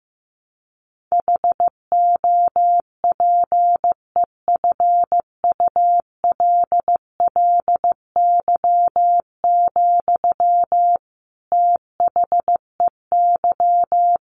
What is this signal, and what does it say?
Morse code, heard as a keyed tone, about 15 wpm, 700 Hz. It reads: HOPEFULLY, THEY
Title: Sentences from Top 1500 Words 15wpm